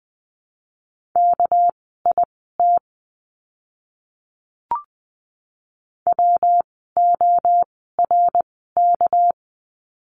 Morse code, heard as a keyed tone, about 20 wpm, spaced as for 15 wpm, 700 Hz. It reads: KIT WORK
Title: 199 Lesson 32f - Letter k - Words 20wpm